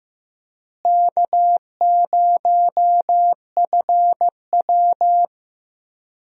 Morse code, heard as a keyed tone, about 15 wpm, 700 Hz. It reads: K0FW
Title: USA Call Signs - 4 Characters - Repeated 3x - 15wpm